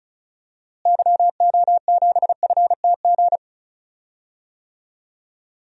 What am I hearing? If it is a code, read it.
Morse code, heard as a keyed tone, about 34 wpm, 700 Hz. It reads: YO7FTG
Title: DXCC-100 Call Signs - 6 Characters - Repeated 3x - 35wpm